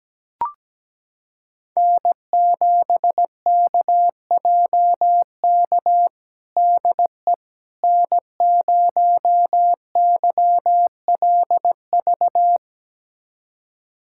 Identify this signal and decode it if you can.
Morse code, heard as a keyed tone, about 17 wpm, 700 Hz. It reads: N7KJK DE N0YLV
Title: USA Callsign DE Callsign - 5 Characters - 17wpm